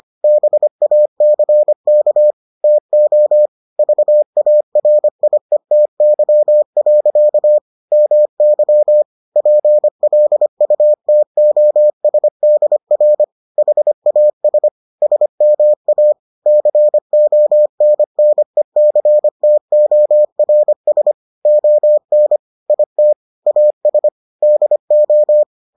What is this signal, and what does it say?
Morse code, heard as a keyed tone, about 25 wpm, 600 Hz. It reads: BACK TO VARIETY. MY PLUTOSDR HAS SMA CONNECTORS ON IT AS DO